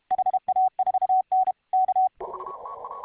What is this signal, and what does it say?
Morse code, heard as a keyed tone, about 22 wpm, 750 Hz. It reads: HA4N K